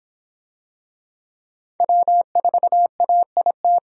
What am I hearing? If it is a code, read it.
Morse code, heard as a keyed tone, about 22 wpm, 700 Hz. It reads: W4AIT